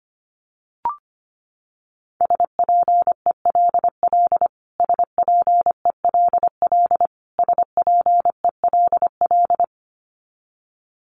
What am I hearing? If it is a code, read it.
Morse code, heard as a keyed tone, about 25 wpm, 700 Hz. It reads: SPELL SPELL SPELL
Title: Top 200 Words Repeated 25wpm